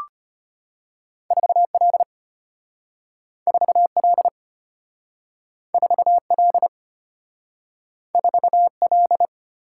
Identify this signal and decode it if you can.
Morse code, heard as a keyed tone, about 33 wpm, 700 Hz. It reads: 4L 4L 4L 4L